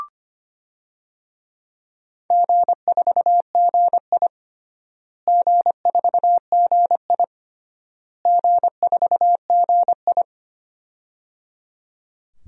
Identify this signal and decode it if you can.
Morse code, heard as a keyed tone, about 25 wpm, 700 Hz. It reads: G4GI G4GI G4GI
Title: DXCC-100 Call Signs - 4 Characters - Repeated 3x - 25wpm